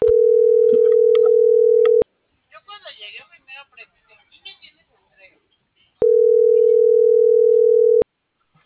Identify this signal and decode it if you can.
Ringback tone (North America), 440 + 480 Hz, with speech.
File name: CENTRAL MAYAB